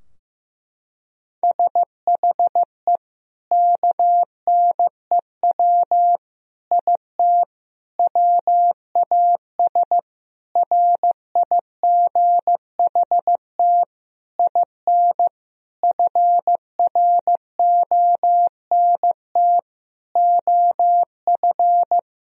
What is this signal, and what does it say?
Morse code, heard as a keyed tone, about 15 wpm, 700 Hz. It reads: SHE KNEW IT WAS RIGHT IN FRONT OF
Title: Sentences from Top 1000 Words - Review Part 2 - 15wpm